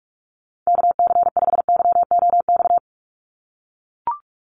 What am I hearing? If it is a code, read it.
Morse code, heard as a keyed tone, about 44 wpm, 700 Hz. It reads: KC5YOX
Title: USA Call Signs 45wpm